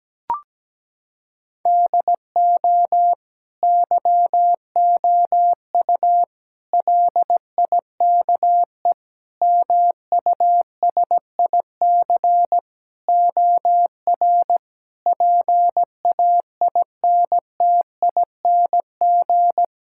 Morse code, heard as a keyed tone, about 17 wpm, 700 Hz. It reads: DO YOU LIKE MUSIC OR PAINTING
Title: Sentences from Top 1000 Words - Review Part 6 - 17wpm